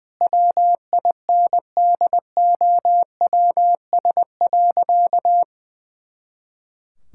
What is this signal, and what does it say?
Morse code, heard as a keyed tone, about 20 wpm, 700 Hz. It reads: WINDOWS.
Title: Pangram Challenge v2 - 20wpm